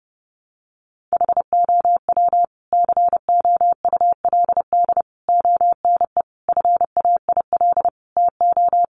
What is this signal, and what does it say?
Morse code, heard as a keyed tone, about 30 wpm, 700 Hz. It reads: HOW COULD ONE FAIL TO
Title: Sentences from Top 1100 Words 30wpm